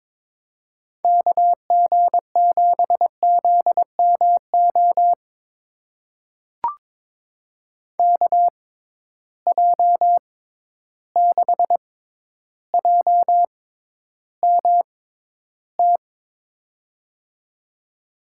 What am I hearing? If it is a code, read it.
Morse code, heard as a keyed tone, about 22 wpm, spaced as for 11 wpm, 700 Hz. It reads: KG7ZMO KJ6JMT
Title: USA Call Signs - 6 Characters - 6x Character Spacing - 22wpm